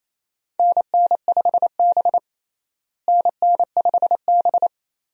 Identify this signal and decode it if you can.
Morse code, heard as a keyed tone, about 28 wpm, 700 Hz. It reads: NN5B NN5B
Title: USA Call Signs - 4 Characters - Repeated 2x - 28wpm